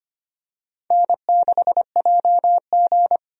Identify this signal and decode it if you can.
Morse code, heard as a keyed tone, about 25 wpm, 700 Hz. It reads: N6JG
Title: USA Call Signs - 4 Characters - Repeated 2x - 25wpm